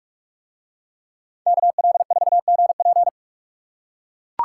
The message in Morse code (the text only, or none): KL4GP